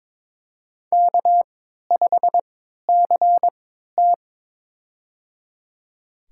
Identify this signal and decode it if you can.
Morse code, heard as a keyed tone, about 22 wpm, spaced as for 13 wpm, 700 Hz. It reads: K5CT
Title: USA Call Signs - 4 Characters - 3x Character Spacing - 22wpm